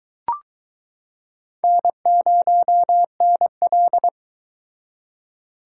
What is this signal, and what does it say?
Morse code, heard as a keyed tone, about 22 wpm, 700 Hz. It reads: N0NL